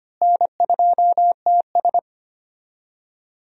Morse code, heard as a keyed tone, about 25 wpm, 700 Hz. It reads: N2TS
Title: USA Call Signs - 4 Characters - Repeated 3x - 25wpm